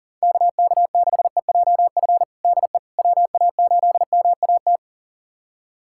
Morse code, heard as a keyed tone, about 40 wpm, 700 Hz. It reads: KK6EJF DE WA8MAT
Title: USA Callsign DE Callsign 40wpm